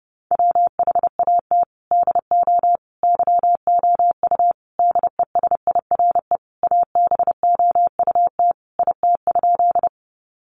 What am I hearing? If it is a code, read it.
Morse code, heard as a keyed tone, about 30 wpm, 700 Hz. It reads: WHAT DO YOU DESIRE ABOUT IT?
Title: Sentences from Top 1100 Words 30wpm